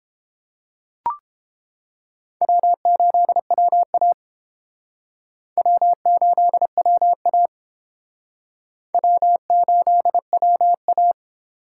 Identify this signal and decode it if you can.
Morse code, heard as a keyed tone, about 30 wpm, 690 Hz. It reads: W8WA W8WA W8WA